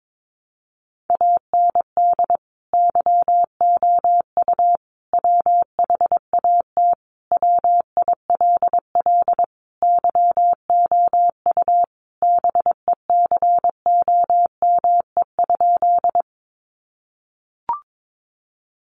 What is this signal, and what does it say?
Morse code, heard as a keyed tone, about 22 wpm, 700 Hz. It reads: AND YOU WHAT WILL YOU BECOME?
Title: Sentences from Top 600 Words 22wpm